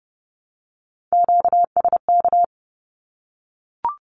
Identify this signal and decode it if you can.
Morse code, heard as a keyed tone, about 30 wpm, 700 Hz. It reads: QSK
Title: Q-Codes Repeated 3x 30wpm